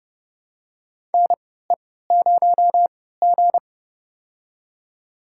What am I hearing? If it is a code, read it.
Morse code, heard as a keyed tone, about 30 wpm, spaced as for 17 wpm, 700 Hz. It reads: NE0G
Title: USA Call Signs - 4 Characters - 3x Character Spacing - 30wpm